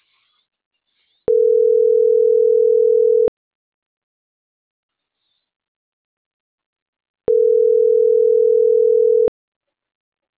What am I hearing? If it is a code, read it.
Ringback tone (North America), 440 + 480 Hz.